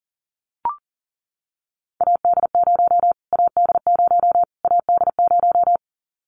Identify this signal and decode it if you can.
Morse code, heard as a keyed tone, about 40 wpm, 700 Hz. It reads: AD0 AD0 AD0
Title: Callsign Prefixes for Top 25 DXCC - Repeated 3x - 40wpm